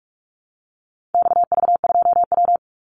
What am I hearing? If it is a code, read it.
Morse code, heard as a keyed tone, about 44 wpm, 700 Hz. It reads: XV2W
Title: Call Signs Repeated 3x - 45wpm